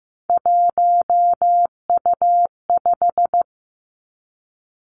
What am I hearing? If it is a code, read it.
Morse code, heard as a keyed tone, about 15 wpm, 700 Hz. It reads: 1U5
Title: Three Challenging Letter-Number Combinations 15wpm